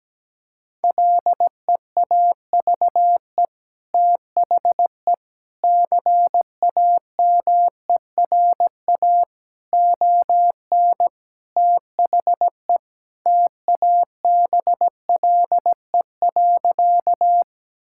Morse code, heard as a keyed tone, about 17 wpm, 700 Hz. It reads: LEAVE THE CAMERA ON THE TABLE.
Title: Sentences from Top 1000 Words - Review Part 5 - 17wpm